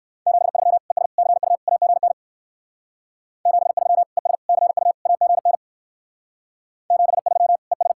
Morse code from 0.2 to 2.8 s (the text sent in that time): 73 ES CU AGN